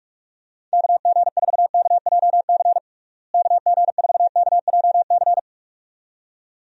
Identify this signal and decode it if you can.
Morse code, heard as a keyed tone, about 42 wpm, 700 Hz. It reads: KG4KJC KG4KJC
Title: USA Call Signs - 6 Characters - Repeated 2x - 45wpm